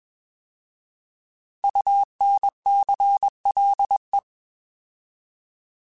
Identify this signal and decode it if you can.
Morse code, heard as a keyed tone, about 21 wpm, 800 Hz. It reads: UNCLE
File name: Alphabetical / Alphabetical mp3s / Us @20